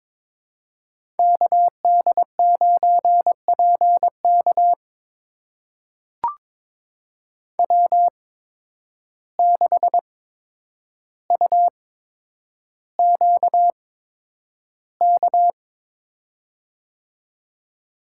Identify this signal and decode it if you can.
Morse code, heard as a keyed tone, about 22 wpm, spaced as for 9 wpm, 700 Hz. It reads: KD9PK W6UQK
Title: USA Call Signs - 5 Characters - 8x Character Spacing - 22wpm